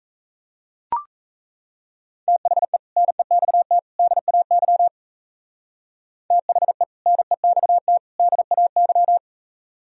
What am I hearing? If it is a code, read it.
Morse code, heard as a keyed tone, about 35 wpm, 700 Hz. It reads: THE NEXT DAY THE NEXT DAY
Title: Speed-Racing - Sets of 3 Words 28wpm